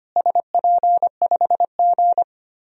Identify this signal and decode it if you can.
Morse code, heard as a keyed tone, about 25 wpm, 700 Hz. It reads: SP5G
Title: DXCC-100 Call Signs - 4 Characters - Repeated 3x - 25wpm